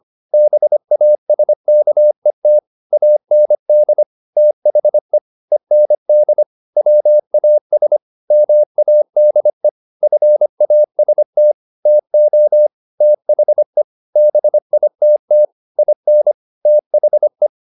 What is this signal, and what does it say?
Morse code, heard as a keyed tone, about 25 wpm, 600 Hz. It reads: BASKET AND THE END WAS MADE FAST TO THE BITT IN THE